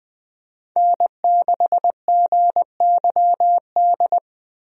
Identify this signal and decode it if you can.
Morse code, heard as a keyed tone, about 20 wpm, 700 Hz. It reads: N6GYD
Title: USA Call Signs - 5 Characters - Repeated 2x - 20wpm